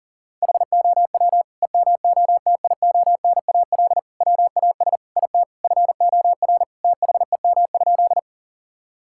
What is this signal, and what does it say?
Morse code, heard as a keyed tone, about 40 wpm, 700 Hz. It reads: HOW EMOTIONAL WAS IT FOR THEM?
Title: Sentences from Top 1000 Words - Review Part 5 - 40wpm